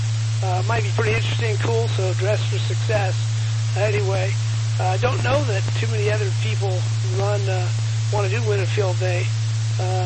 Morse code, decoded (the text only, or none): none